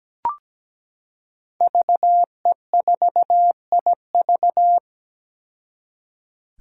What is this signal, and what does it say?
Morse code, heard as a keyed tone, about 17 wpm, 700 Hz. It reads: VE4IV
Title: Five Challenging Letter-Number Combinations 17wpm